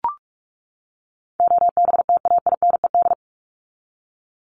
Morse code, heard as a keyed tone, about 44 wpm, 700 Hz. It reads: OBTAINED
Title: ED Suffix - Top 100 Words with Base Words 45wpm